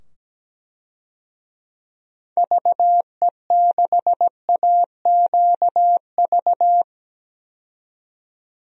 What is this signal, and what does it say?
Morse code, heard as a keyed tone, about 17 wpm, 700 Hz. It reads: VE6AQV
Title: DXCC-100 Call Signs - 6 Characters - 17wpm